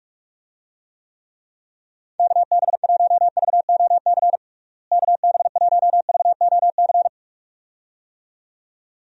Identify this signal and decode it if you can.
Morse code, heard as a keyed tone, about 43 wpm, 700 Hz. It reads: KB1VOC KB1VOC
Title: USA Call Signs - 6 Characters - Repeated 2x - 45wpm